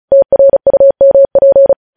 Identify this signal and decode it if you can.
Morse code, heard as a keyed tone, about 36 wpm, 550 Hz. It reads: TRUMP
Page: News Headlines in Morse Code at 35 WPM